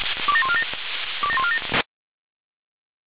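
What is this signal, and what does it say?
Five-tone selective call (ZVEI), heard as quick runs of five tones, tones of about 70 ms. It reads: 27147 27147
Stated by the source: ZVEI five-tone call 27001 received by SDR.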